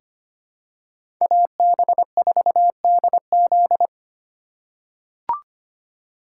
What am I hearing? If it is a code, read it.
Morse code, heard as a keyed tone, about 25 wpm, 700 Hz. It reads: AB4DZ
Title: USA Call Signs - 3x Character Spacing - 25wpm